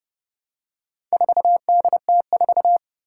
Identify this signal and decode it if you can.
Morse code, heard as a keyed tone, about 31 wpm, 700 Hz. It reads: 4DT4
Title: Four Challenging Letter-Number Combinations 30wpm